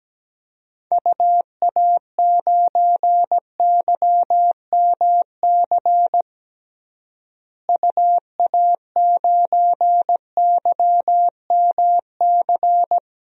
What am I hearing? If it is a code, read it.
Morse code, heard as a keyed tone, about 17 wpm, 700 Hz. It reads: UA9YMC UA9YMC
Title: DXCC-100 Call Signs - 6 Characters - Repeated 2x - 17wpm